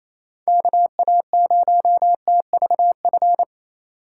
Morse code, heard as a keyed tone, about 28 wpm, 700 Hz. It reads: KA0TVF
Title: USA Call Signs - 6 Characters - 4x Character Spacing - 28wpm